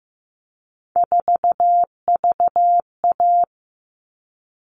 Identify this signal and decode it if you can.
Morse code, heard as a keyed tone, about 15 wpm, 700 Hz. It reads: 4VA
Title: All Cut Numbers - 3-Digit 15wpm